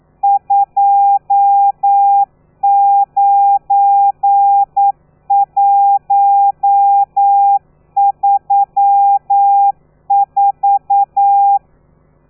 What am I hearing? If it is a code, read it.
Morse code, heard as a keyed tone, about 9 wpm, 800 Hz. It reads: 29134
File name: telegrafia / i2tfj / Lez12 40